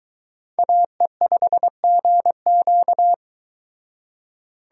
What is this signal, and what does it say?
Morse code, heard as a keyed tone, about 23 wpm, 700 Hz. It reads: AE5GQ